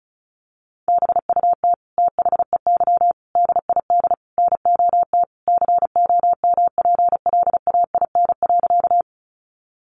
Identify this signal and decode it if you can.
Morse code, heard as a keyed tone, about 34 wpm, 700 Hz. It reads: BUT THEY DID NOT COMPLAIN.
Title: Sentences from Top 1300 Words 35wpm